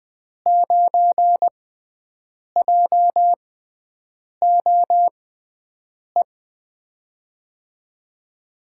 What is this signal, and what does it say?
Morse code, heard as a keyed tone, about 20 wpm, spaced as for 7 wpm, 700 Hz. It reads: 9JOE
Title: USA Call Signs - 6x Character Spacing - 20wpm